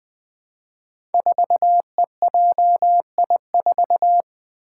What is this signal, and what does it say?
Morse code, heard as a keyed tone, about 20 wpm, 700 Hz. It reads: 4EJI4
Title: Five Challenging Letter-Number Combinations 20wpm